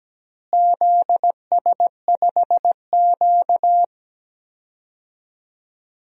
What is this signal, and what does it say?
Morse code, heard as a keyed tone, about 17 wpm, 700 Hz. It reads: ZS5Q
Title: DXCC-100 Call Signs - 4 Characters - Repeated 2x - 17wpm